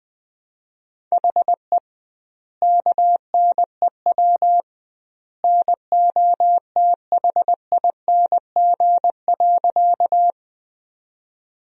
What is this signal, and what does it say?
Morse code, heard as a keyed tone, about 20 wpm, 700 Hz. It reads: HE KNEW NOTHING.
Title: 2x Word-Spacing - Sentences from Top 500 Words - 20wpm